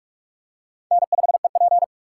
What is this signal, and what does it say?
Morse code, heard as a keyed tone, about 41 wpm, 700 Hz. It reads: N5EP
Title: USA Call Signs - 4 Characters - 4x Character Spacing - 45wpm